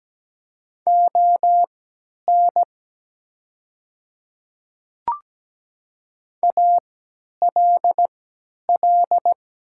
Morse code, heard as a keyed tone, about 17 wpm, spaced as for 10 wpm, 700 Hz. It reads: ON ALL
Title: 3x Character Spacing - Top 100 Words - 17wpm